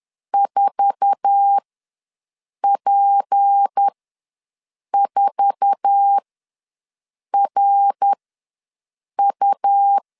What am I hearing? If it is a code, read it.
Morse code, heard as a keyed tone, about 10 wpm, spaced as for 6 wpm, 800 Hz. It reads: 4P4RU
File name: LEZIONI / 8 YJ45 / 8 YJ4 ws 10wpm 4020